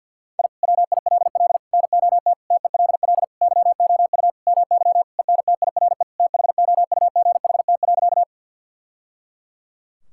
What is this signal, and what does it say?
Morse code, heard as a keyed tone, about 51 wpm, 690 Hz. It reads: I WILL NOT TELL YOU MY ENTIRE THOUGHT.
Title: Sentences from Top 1000 Words - Review Part 5 - 50wpm